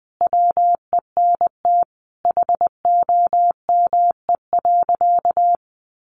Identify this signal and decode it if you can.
Morse code, heard as a keyed tone, about 20 wpm, 700 Hz. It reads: WENT HOME.
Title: Sentences from Top 1200 Words 20wpm